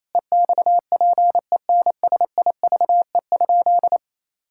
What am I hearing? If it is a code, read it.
Morse code, heard as a keyed tone, about 28 wpm, 700 Hz. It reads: EXPENSIVE?
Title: Sentences from Top 1000 Words - Review Part 6 - 28wpm